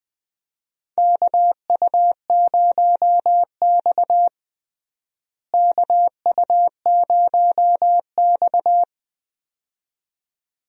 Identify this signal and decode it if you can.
Morse code, heard as a keyed tone, about 20 wpm, 700 Hz. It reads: KU0X KU0X
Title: USA Call Signs - 4 Characters - Repeated 2x - 20wpm